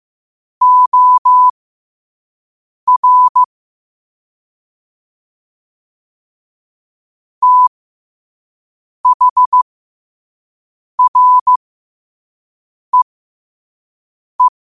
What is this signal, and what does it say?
Morse code, heard as a keyed tone, about 15 wpm, spaced as for 5 wpm, 1000 Hz. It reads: OR THREE
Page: New quotes every day in morse code at 5 Words per minute.